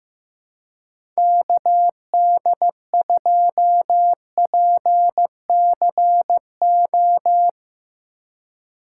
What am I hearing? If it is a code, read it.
Morse code, heard as a keyed tone, about 15 wpm, 700 Hz. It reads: KD2PCO
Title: USA Call Signs - 6 Characters - 6x Character Spacing - 15wpm